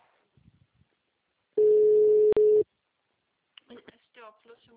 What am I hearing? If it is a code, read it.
Ringback tone (Europe, 425 Hz), with speech.
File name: out- -2031- - - .41244